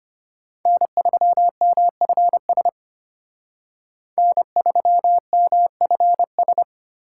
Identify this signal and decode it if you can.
Morse code, heard as a keyed tone, about 33 wpm, 700 Hz. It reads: N3MFS N3MFS